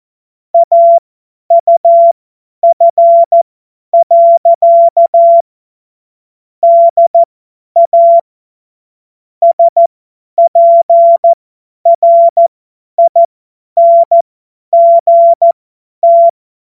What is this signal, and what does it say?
Morse code, heard as a keyed tone, about 14 wpm, spaced as for 10 wpm, 680 Hz. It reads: AUF. DA SPRINGT
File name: Witz-28-Oktober-2025-50-F70